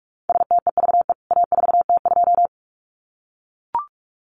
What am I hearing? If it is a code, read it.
Morse code, heard as a keyed tone, about 44 wpm, 700 Hz. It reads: STEVE A4T2
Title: CWT Exchange 45wpm